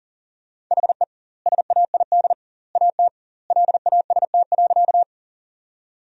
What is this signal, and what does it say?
Morse code, heard as a keyed tone, about 40 wpm, 700 Hz. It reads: HE SAID AT LAST.
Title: 2x Word-Spacing - Sentences from Top 300 Words - 40wpm